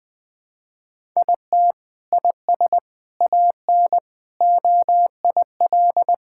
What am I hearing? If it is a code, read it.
Morse code, heard as a keyed tone, about 20 wpm, 700 Hz. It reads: IT IS AN OIL
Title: Sentences from Top 1000 Words - Review Part 6 - 20wpm